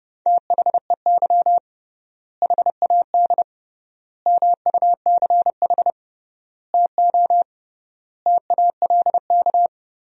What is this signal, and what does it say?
Morse code, heard as a keyed tone, about 30 wpm, 700 Hz. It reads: THEY HAD MUCH TO TALK
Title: 3x Word-Spacing - Sentences from Top 400 Words - 30wpm